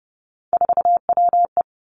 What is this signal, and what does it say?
Morse code, heard as a keyed tone, about 30 wpm, 700 Hz. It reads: 4WE
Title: Three Challenging Letter-Number Combinations 30wpm